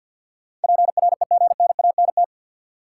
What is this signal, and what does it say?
Morse code, heard as a keyed tone, about 47 wpm, 700 Hz. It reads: PREGNANT
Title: Sentences from Top 1000 Words - Review Part 6 - New Word Drill - 50wpm